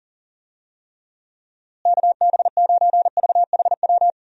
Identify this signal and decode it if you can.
Morse code, heard as a keyed tone, about 40 wpm, 700 Hz. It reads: KB9VHW
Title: USA Call Signs - 6 Characters - Repeated 2x - 40wpm